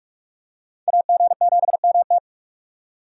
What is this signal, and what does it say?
Morse code, heard as a keyed tone, about 41 wpm, 700 Hz. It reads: AG7MT